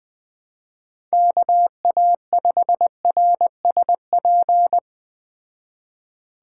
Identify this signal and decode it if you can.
Morse code, heard as a keyed tone, about 20 wpm, 700 Hz. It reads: KA5RSP